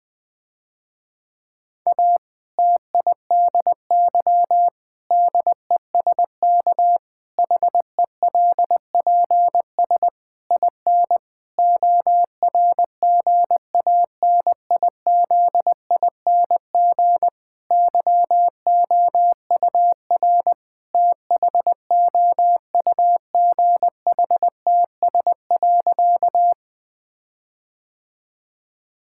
Morse code, heard as a keyed tone, about 20 wpm, 700 Hz. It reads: A TIDY DESK HELPS IN ORGANIZING YOUR THOUGHTS.
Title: Difficult Sentences with Hints - 20wpm